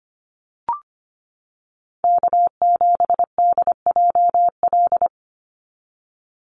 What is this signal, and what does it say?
Morse code, heard as a keyed tone, about 25 wpm, 700 Hz. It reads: K7DJL
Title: NA SOTA Chaser Callsigns 25wpm